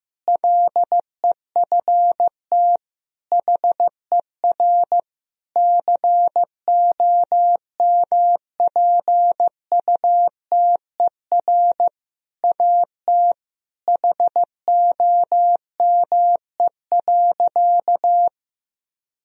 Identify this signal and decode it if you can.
Morse code, heard as a keyed tone, about 15 wpm, 700 Hz. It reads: LEFT HER COMPUTER AT HOME.
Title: Sentences from Top 1000 Words - Review Part 3 - 15wpm